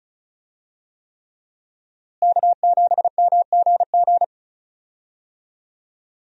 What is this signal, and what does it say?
Morse code, heard as a keyed tone, about 36 wpm, 700 Hz. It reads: K7MGG
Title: USA Call Signs - 5 Characters - Repeated 3x - 35wpm